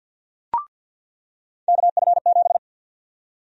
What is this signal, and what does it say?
Morse code, heard as a keyed tone, about 47 wpm, 700 Hz. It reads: KF7
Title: USA Call Sign Prefixes - 50wpm